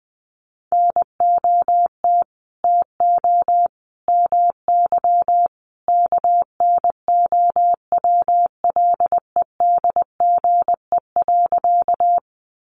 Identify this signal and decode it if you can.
Morse code, heard as a keyed tone, about 20 wpm, 700 Hz. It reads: NOT TO MY KNOWLEDGE.
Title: Sentences from Top 600 Words 20wpm